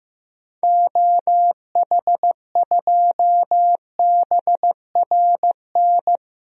Morse code, heard as a keyed tone, about 15 wpm, 700 Hz. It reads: OH2BRN
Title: DXCC-100 Call Signs - 6 Characters - Repeated 3x - 15wpm